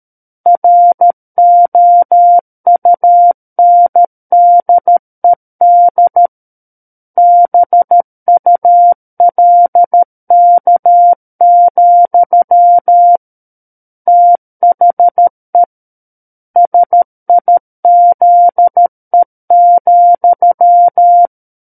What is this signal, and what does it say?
Morse code, heard as a keyed tone, about 13 wpm, 700 Hz. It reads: ROUNDED BULK, THE SIZE,
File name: Book One / 04-Chapter 4 - 13 WPM